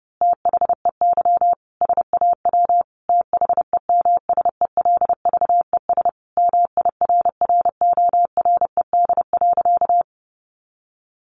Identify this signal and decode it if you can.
Morse code, heard as a keyed tone, about 30 wpm, 700 Hz. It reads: THEY SAW THEMSELVES MIRRORED.
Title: A Sound Of Thunder 30wpm